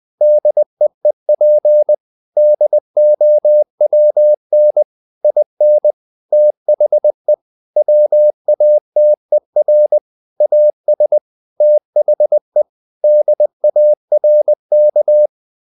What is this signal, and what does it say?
Morse code, heard as a keyed tone, about 20 wpm, 600 Hz. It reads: DEEP DOWN IN THE WATER AS THE DARK
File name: the old man and the sea part 2 20wpm